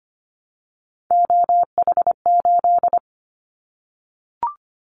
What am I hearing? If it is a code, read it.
Morse code, heard as a keyed tone, about 25 wpm, 700 Hz. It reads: OH8
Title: Callsign Prefixes for Top 25 DXCC - 25wpm